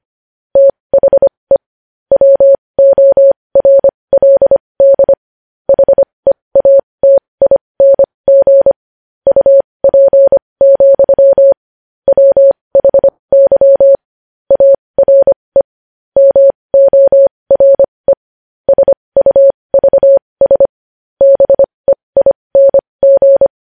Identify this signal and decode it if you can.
Morse code, heard as a keyed tone, about 25 wpm, spaced as for 20 wpm, 550 Hz. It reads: THE WORLD HEATING UP, WHY ARE MORE SUVS BEING
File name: podcast / News headlines-1x-25x20